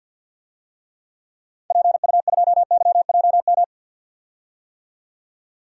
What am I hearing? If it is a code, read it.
Morse code, heard as a keyed tone, about 50 wpm, 700 Hz. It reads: PU2YJM
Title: DXCC-100 Call Signs - 6 Characters - 50wpm